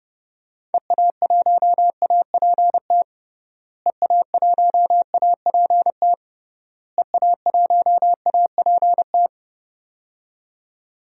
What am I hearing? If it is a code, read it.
Morse code, heard as a keyed tone, about 32 wpm, 700 Hz. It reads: EA1APT EA1APT EA1APT
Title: DXCC-100 Call Signs - 6 Characters - Repeated 3x - 30wpm